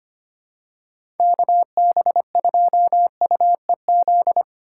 Morse code, heard as a keyed tone, about 25 wpm, 700 Hz. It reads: KB2UEZ
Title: USA Call Signs - 6 Characters - 4x Character Spacing - 25wpm